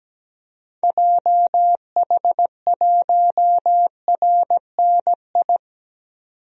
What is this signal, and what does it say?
Morse code, heard as a keyed tone, about 17 wpm, 700 Hz. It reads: JH1RNI
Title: DXCC-100 Call Signs - 6 Characters - 17wpm